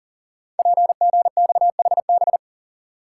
Morse code, heard as a keyed tone, about 40 wpm, 700 Hz. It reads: PGXHB
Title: Code Groups - 5 Letters - 40wpm